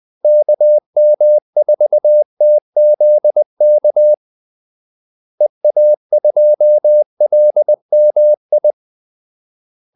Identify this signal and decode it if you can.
Morse code, heard as a keyed tone, about 20 wpm, 600 Hz. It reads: KM4TZK EA2LMI
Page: Kaldesignaler 20 wpm